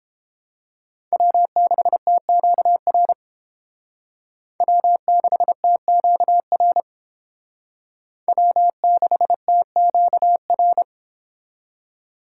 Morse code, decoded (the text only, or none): W6TQR W6TQR W6TQR